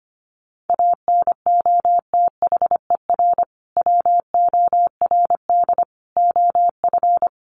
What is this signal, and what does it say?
Morse code, heard as a keyed tone, about 25 wpm, 700 Hz. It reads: ANOTHER WORD OF
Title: Sentences from Top 1100 Words 25wpm